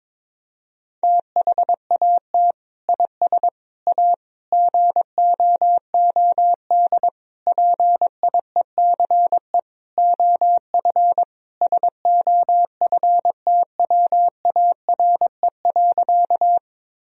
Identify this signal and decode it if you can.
Morse code, heard as a keyed tone, about 21 wpm, 700 Hz. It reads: THAT IS A GOOD PIECE OF SOFTWARE.
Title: Sentences from Top 1000 Words - Review Part 3 - 22wpm